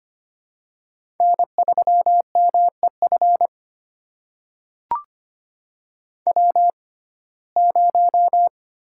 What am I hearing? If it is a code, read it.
Morse code, heard as a keyed tone, about 25 wpm, spaced as for 18 wpm, 700 Hz. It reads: N3MEF W0
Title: USA Call Signs - 5 Characters - 6x Character Spacing - 25wpm